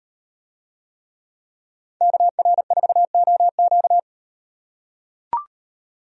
Morse code, heard as a keyed tone, about 36 wpm, 700 Hz. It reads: KR4OQ